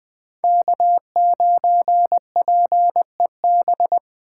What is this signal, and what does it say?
Morse code, heard as a keyed tone, about 20 wpm, 700 Hz. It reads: K9PEB
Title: USA Call Signs - 5 Characters - 2x Character Spacing - 20wpm